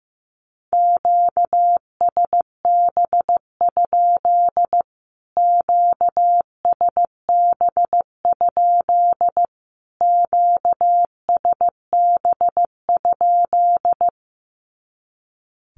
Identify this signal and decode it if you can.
Morse code, heard as a keyed tone, about 15 wpm, 700 Hz. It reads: QSB? QSB? QSB?
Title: Q-Codes Repeated 3x 15wpm